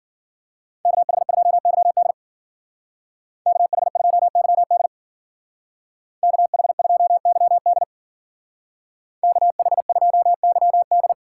Morse code, heard as a keyed tone, about 50 wpm, 700 Hz. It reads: KH2YD KH2YD KH2YD KH2YD
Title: Speed Racing - USA 2x2 Call Signs 40wpm